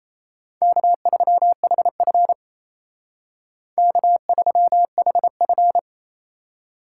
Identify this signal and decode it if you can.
Morse code, heard as a keyed tone, about 35 wpm, 700 Hz. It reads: K3HF K3HF
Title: Speed Racing - USA Call Signs 28wpm